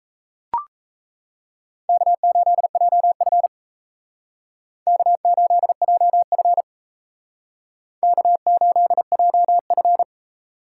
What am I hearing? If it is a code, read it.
Morse code, heard as a keyed tone, about 38 wpm, 700 Hz. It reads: K8JF K8JF K8JF